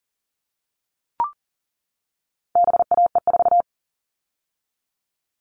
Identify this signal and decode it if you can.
Morse code, heard as a keyed tone, about 40 wpm, 700 Hz. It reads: BAE4
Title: All Cut Numbers - 4-Digit 40wpm